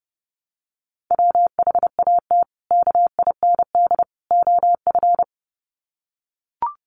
Morse code, heard as a keyed tone, about 30 wpm, 700 Hz. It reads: WHAT KIND OF
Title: Sets of 3 Words 30wpm